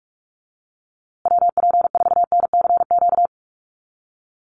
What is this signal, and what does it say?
Morse code, heard as a keyed tone, about 44 wpm, 700 Hz. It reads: WP4NCQ
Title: USA Call Signs Repeated 2x 45wpm